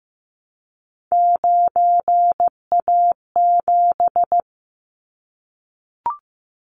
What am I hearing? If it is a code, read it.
Morse code, heard as a keyed tone, about 15 wpm, 700 Hz. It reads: 9A7
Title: Callsign Prefixes for Top 25 DXCC - 15wpm